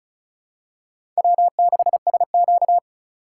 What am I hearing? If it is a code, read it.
Morse code, heard as a keyed tone, about 34 wpm, 700 Hz. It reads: W6SQ
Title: USA Call Signs - 4 Characters - 2x Character Spacing - 35wpm